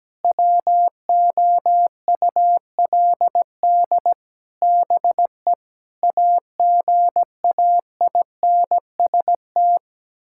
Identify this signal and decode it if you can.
Morse code, heard as a keyed tone, about 17 wpm, 700 Hz. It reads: WOULD BE AGAINST
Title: Sentences from Top 1000 Words - Review Part 6 - 17wpm